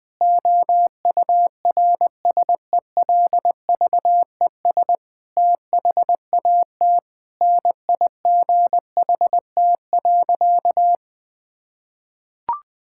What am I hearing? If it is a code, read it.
Morse code, heard as a keyed tone, about 20 wpm, 700 Hz. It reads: OURSELVES THAT NIGHT.
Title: Sentences from Top 1600 Words 20wpm